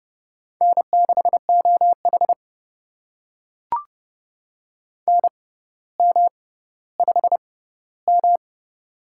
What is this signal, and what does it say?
Morse code, heard as a keyed tone, about 32 wpm, spaced as for 16 wpm, 700 Hz. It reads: N6OH NM5M
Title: USA Call Signs - 4 Characters - 6x Character Spacing - 30wpm